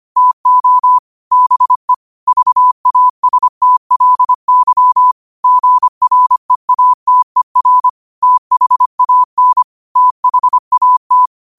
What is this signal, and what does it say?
Morse code, heard as a keyed tone, about 25 wpm, 1000 Hz. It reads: TO BE VASTLY GREATER THAN THAT
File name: qotdfiles / 25wpm